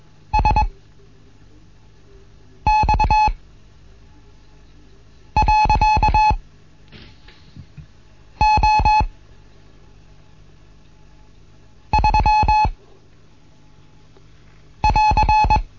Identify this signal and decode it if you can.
Morse code, heard as a keyed tone, about 21 wpm, spaced as for 4 wpm, 860 Hz. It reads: SX.O3<AR>
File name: Morsen / Kurs A 15